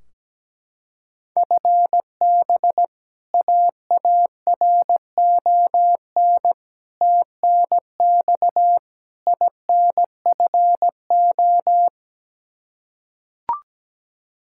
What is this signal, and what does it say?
Morse code, heard as a keyed tone, about 17 wpm, 700 Hz. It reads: FB AARON TNX INFO
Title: Practice QSO - Exchange 3 - More Info - 17wpm